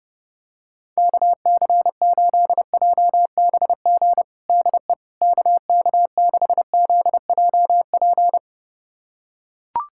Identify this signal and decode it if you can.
Morse code, heard as a keyed tone, about 30 wpm, 700 Hz. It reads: KC8JBG DE KK6ZJP
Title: USA Callsign DE Callsign - 6 Characters - 30wpm